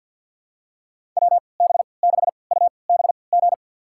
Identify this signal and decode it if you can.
Morse code, heard as a keyed tone, about 52 wpm, spaced as for 28 wpm, 700 Hz. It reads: WB6UBG
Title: USA Call Signs - 6 Characters - 3x Character Spacing - 50wpm